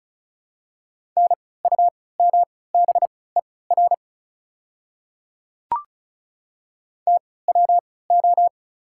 Morse code, heard as a keyed tone, about 36 wpm, spaced as for 20 wpm, 690 Hz. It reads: NUMBER TWO
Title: 3x Character Spacing - Top 100 Words - 35wpm